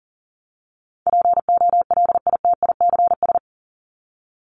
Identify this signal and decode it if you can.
Morse code, heard as a keyed tone, about 38 wpm, 700 Hz. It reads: POLITICS
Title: Top 751-1000 Words 40wpm